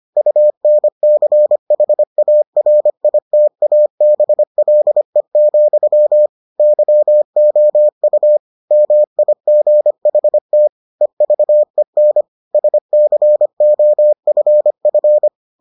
Morse code, heard as a keyed tone, about 25 wpm, 600 Hz. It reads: UNCHARITABLE, YOU MIGHT EVEN SCOFF